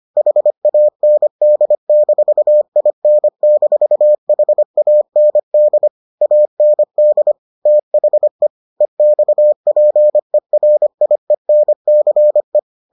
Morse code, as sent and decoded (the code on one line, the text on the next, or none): .... .- -. -.. -....- .. -. -....- .... .- -. -.. / .- -. -.. / - .... / . -..- .--. .-. .. -. -.-. .
HAND-IN-HAND AND THE EXPERIENCE